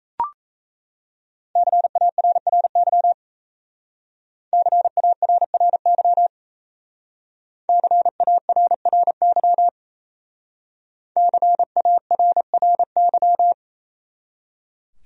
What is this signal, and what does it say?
Morse code, heard as a keyed tone, about 36 wpm, 700 Hz. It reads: CARRY CARRY CARRY CARRY